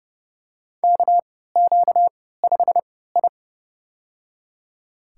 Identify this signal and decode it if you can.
Morse code, heard as a keyed tone, about 31 wpm, spaced as for 17 wpm, 700 Hz. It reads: KQ5I
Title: USA Call Signs - 4 Characters - 3x Character Spacing - 30wpm